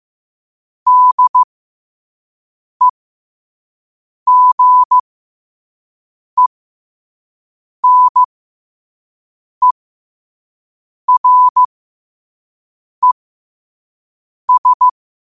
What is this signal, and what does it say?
Morse code, heard as a keyed tone, about 15 wpm, spaced as for 5 wpm, 1000 Hz. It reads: DEGENERES
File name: qotdfiles / 5wpm